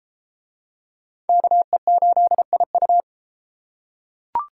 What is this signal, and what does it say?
Morse code, heard as a keyed tone, about 32 wpm, 700 Hz. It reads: KE8IU